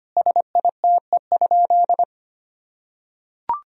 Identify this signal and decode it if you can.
Morse code, heard as a keyed tone, about 25 wpm, 700 Hz. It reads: SITE?
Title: Sentences from Top 1000 Words - Review Part 4 - 25wpm